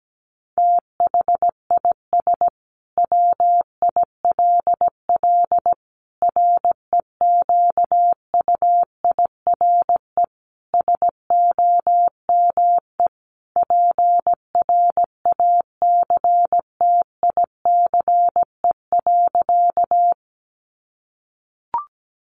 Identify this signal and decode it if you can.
Morse code, heard as a keyed tone, about 17 wpm, 700 Hz. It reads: THIS WILL REQUIRE SOME PRACTICE.
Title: Sentences from Top 900 Words 17wpm